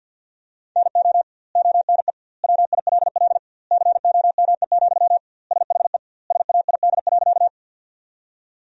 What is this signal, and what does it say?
Morse code, heard as a keyed tone, about 47 wpm, 700 Hz. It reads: NO ONE WILL COME, SHE SAID.
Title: 2x Word-Spacing - Sentences from Top 100 Words - 50wpm